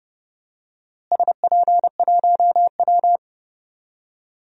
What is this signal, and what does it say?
Morse code, heard as a keyed tone, about 30 wpm, 700 Hz. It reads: SP1W
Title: DXCC-100 Call Signs - 4 Characters - Repeated 3x - 30wpm